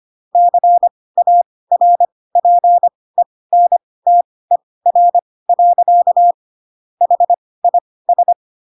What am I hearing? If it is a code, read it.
Morse code, heard as a keyed tone, about 25 wpm, spaced as for 18 wpm, 700 Hz. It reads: CARPENTER. HIS
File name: Pinocchio / Pinocchio ch1 12 25 ik1hge